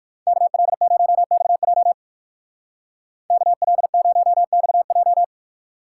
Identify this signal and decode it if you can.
Morse code, heard as a keyed tone, about 59 wpm, 700 Hz. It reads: KL0XJ KL0XJ